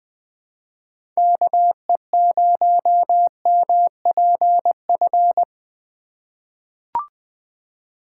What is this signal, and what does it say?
Morse code, heard as a keyed tone, about 20 wpm, 700 Hz. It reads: KE0MPF
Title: USA Call Signs - 6 Characters - 4x Character Spacing - 20wpm